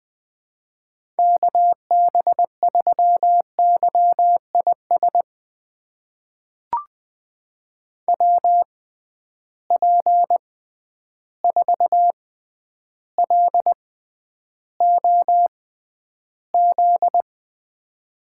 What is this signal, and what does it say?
Morse code, heard as a keyed tone, about 20 wpm, spaced as for 10 wpm, 700 Hz. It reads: KB3YIS WP4LOZ
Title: USA Call Signs - 6 Characters - 6x Character Spacing - 20wpm